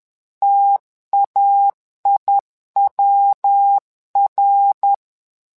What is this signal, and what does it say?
Morse code, heard as a keyed tone, about 10 wpm, 800 Hz. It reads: TAIWR
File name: LEZIONI / 3 MWRO / 3 MWRO 10wpm 2148